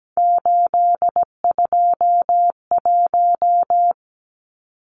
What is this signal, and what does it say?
Morse code, heard as a keyed tone, about 17 wpm, 700 Hz. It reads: 821
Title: Numbers - 3-Digit 17wpm